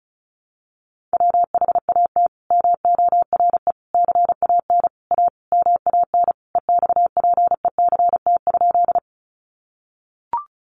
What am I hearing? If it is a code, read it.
Morse code, heard as a keyed tone, about 34 wpm, 700 Hz. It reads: WHAT MORE CAN A MAN EXPECT?
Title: Sentences from Top 1100 Words 35wpm